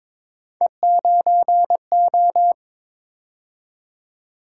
Morse code, heard as a keyed tone, about 22 wpm, 700 Hz. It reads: E9O
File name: practice / 22wpm / Three Letter-Number Combinations - 22wpm